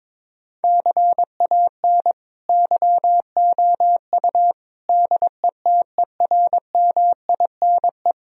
Morse code, heard as a keyed tone, about 22 wpm, 700 Hz. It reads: CAN YOU DETERMINE
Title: Sentences from Top 1000 Words - Review Part 4 - 22wpm